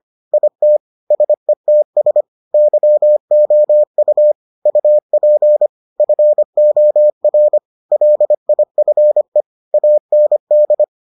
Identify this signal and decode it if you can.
Morse code, heard as a keyed tone, about 25 wpm, 600 Hz. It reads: IT SETS YOU UP FOR LIFE AND